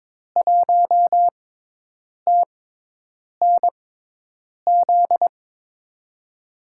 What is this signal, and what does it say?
Morse code, heard as a keyed tone, about 22 wpm, spaced as for 8 wpm, 700 Hz. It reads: 1TNZ
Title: USA Call Signs - 5 Characters - 6x Character Spacing - 22wpm